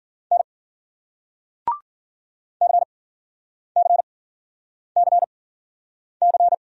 Morse code, heard as a keyed tone, about 50 wpm, spaced as for 9 wpm, 700 Hz. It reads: N CCCC
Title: Speed-Racing Single Letter-Number 40wpm